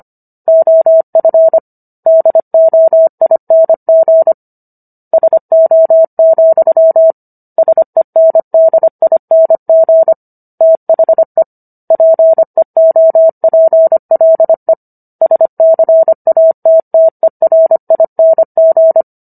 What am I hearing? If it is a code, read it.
Morse code, heard as a keyed tone, about 25 wpm, 650 Hz. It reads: OF DOING SO, SENDING THE PEOPLE SCATTERING